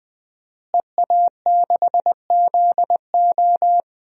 Morse code, heard as a keyed tone, about 20 wpm, 700 Hz. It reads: EA6ZO
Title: DXCC-100 Call Signs - 5 Characters - Repeated 3x - 20wpm